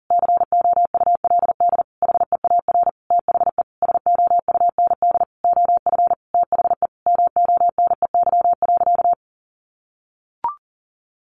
Morse code, heard as a keyed tone, about 40 wpm, 700 Hz. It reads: COULD HEAR THE SOUND OF THE MONEY.
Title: Sentences from Top 1100 Words 40wpm